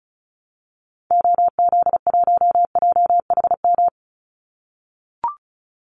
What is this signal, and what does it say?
Morse code, heard as a keyed tone, about 34 wpm, 700 Hz. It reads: OZ1JHM
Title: Call Signs - 35wpm